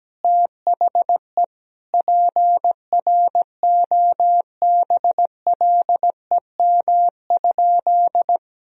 Morse code, heard as a keyed tone, about 17 wpm, 700 Hz. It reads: THE PROBLEM?
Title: Sentences from Top 1000 Words - Review Part 4 - 17wpm